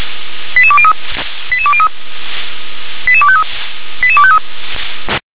Five-tone selective call (ZVEI), heard as quick runs of five tones, tones of about 70 ms. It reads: 88203 88203 88253 88253
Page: ZVEI